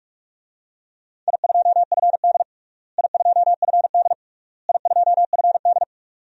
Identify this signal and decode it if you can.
Morse code, heard as a keyed tone, about 46 wpm, 700 Hz. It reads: I2FD I2FD I2FD
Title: DXCC-100 Call Signs - 4 Characters - Repeated 3x - 45wpm